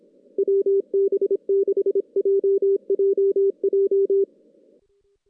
Morse code, heard as a keyed tone, about 25 wpm, 390 Hz, with radio noise. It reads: WB6JJJ
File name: 2016 JAN CW NAQP / 2016 Jan naqp k5ot - 0-6